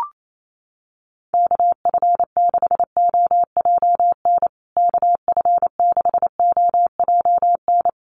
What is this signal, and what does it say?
Morse code, heard as a keyed tone, about 28 wpm, 700 Hz. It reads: KF6OJN KF6OJN
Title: USA Call Signs Repeated 2x 28wpm